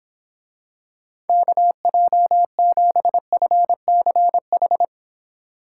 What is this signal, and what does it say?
Morse code, heard as a keyed tone, about 26 wpm, 700 Hz. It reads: KJ7FCH